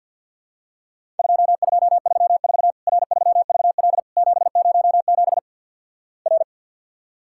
Morse code, heard as a keyed tone, about 50 wpm, 700 Hz. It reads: 2234 R3VL 707 R
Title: Worked All Europe DX Contest - QTC Exchange with Repeats - 50wpm